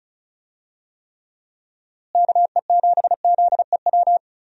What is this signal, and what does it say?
Morse code, heard as a keyed tone, about 34 wpm, 700 Hz. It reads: KE7ZEW
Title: USA Call Signs - 6 Characters - Repeated 3x - 35wpm